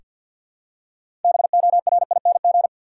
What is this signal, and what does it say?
Morse code, heard as a keyed tone, about 47 wpm, 700 Hz. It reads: BORING
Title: Six Letter Words - 50wpm